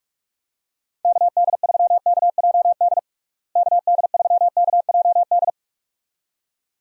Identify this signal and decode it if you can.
Morse code, heard as a keyed tone, about 46 wpm, 700 Hz. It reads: KD3KJD KD3KJD
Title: USA Call Signs - 6 Characters - Repeated 2x - 45wpm